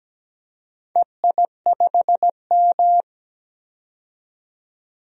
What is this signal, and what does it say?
Morse code, heard as a keyed tone, about 17 wpm, 700 Hz. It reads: EI5M
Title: DXCC-100 Call Signs - 4 Characters - 17wpm